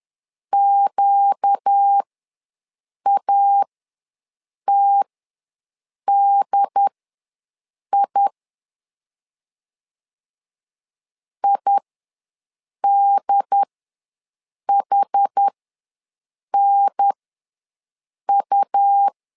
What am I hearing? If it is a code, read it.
Morse code, heard as a keyed tone, about 10 wpm, spaced as for 6 wpm, 800 Hz. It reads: QATDI IDHNU
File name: LEZIONI / 4 Q ws 10wpm 2306